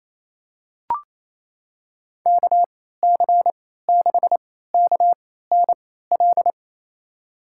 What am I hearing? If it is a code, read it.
Morse code, heard as a keyed tone, about 28 wpm, spaced as for 16 wpm, 700 Hz. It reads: KC6KNL
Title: USA Call Signs - 6 Characters - 3x Character Spacing - 28wpm